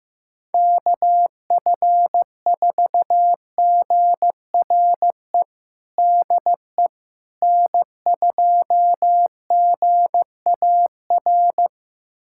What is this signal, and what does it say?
Morse code, heard as a keyed tone, about 16 wpm, 700 Hz. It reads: KF4GRE DE N2GAR
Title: USA Callsign DE Callsign 15wpm